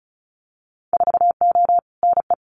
Morse code, heard as a keyed tone, about 36 wpm, 700 Hz. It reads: 4O NE
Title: Winter Field Day Exchanges Encore 35wpm